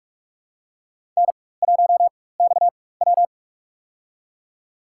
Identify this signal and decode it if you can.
Morse code, heard as a keyed tone, about 44 wpm, spaced as for 21 wpm, 700 Hz. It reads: N1XW
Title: USA Call Signs - 4 Characters - 4x Character Spacing - 45wpm